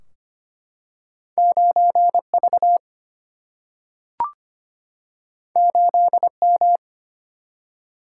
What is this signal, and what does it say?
Morse code, heard as a keyed tone, about 25 wpm, 700 Hz. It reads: 9V 8M
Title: Single-Letter Single-Number 25wpm